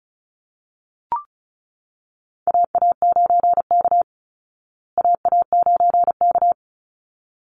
Morse code, heard as a keyed tone, about 34 wpm, 700 Hz. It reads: AA9K AA9K
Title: Call Signs Repeated 3x - 35wpm